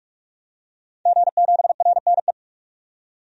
Morse code, heard as a keyed tone, about 46 wpm, 700 Hz. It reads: G7RNE
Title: DXCC-100 Call Signs - 5 Characters - 45wpm